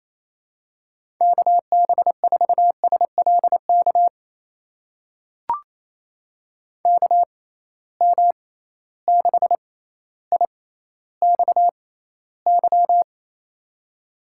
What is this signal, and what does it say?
Morse code, heard as a keyed tone, about 27 wpm, spaced as for 14 wpm, 700 Hz. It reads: KB4SLK KM6IXY
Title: USA Call Signs - 6 Characters - 6x Character Spacing - 28wpm